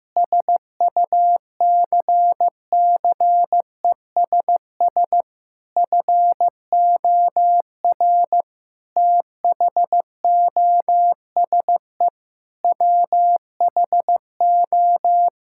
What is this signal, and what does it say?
Morse code, heard as a keyed tone, about 15 wpm, 700 Hz. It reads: SUCCESS FOR THOSE WHO
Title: Sentences from Top 1000 Words - Review Part 6 - 15wpm